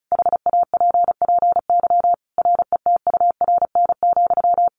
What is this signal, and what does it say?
Morse code, heard as a keyed tone, about 34 wpm, 700 Hz. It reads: HAPPY RETURN,
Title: Sentences from Top 1000 Words 35wpm